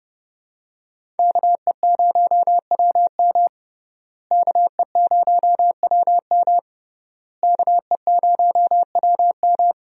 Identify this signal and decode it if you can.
Morse code, heard as a keyed tone, about 31 wpm, 700 Hz. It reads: KE0WM KE0WM KE0WM
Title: USA Call Signs - 5 Characters - Repeated 3x - 30wpm